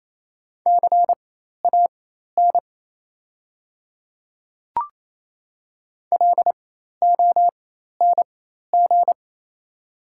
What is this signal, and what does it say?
Morse code, heard as a keyed tone, about 28 wpm, spaced as for 13 wpm, 700 Hz. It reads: CAN LONG
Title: 4x Character Spacing - Top 100 Words - 28wpm